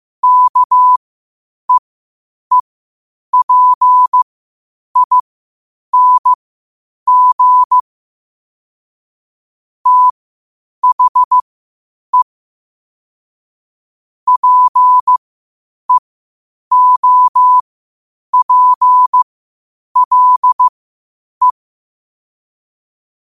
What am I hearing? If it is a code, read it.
Morse code, heard as a keyed tone, about 15 wpm, spaced as for 8 wpm, 1000 Hz. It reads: KEEPING THE PEOPLE
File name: qotdfiles / 8wpm